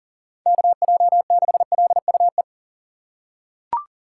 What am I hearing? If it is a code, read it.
Morse code, heard as a keyed tone, about 37 wpm, 700 Hz. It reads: KJ6LUE
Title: USA Call Signs - 6 Characters - 4x Character Spacing - 40wpm